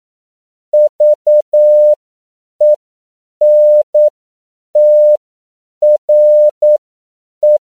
Morse code, heard as a keyed tone, about 9 wpm, spaced as for 7 wpm, 600 Hz. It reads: VENTRE
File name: CW / 10PPM